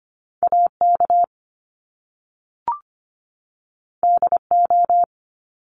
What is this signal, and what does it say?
Morse code, heard as a keyed tone, about 25 wpm, 700 Hz. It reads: AK DO
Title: Top Two Letter Combinations in Words 25wpm